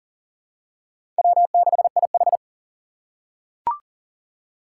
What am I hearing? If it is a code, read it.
Morse code, heard as a keyed tone, about 37 wpm, 700 Hz. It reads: W6IH
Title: USA Call Signs - 4 Characters - 4x Character Spacing - 40wpm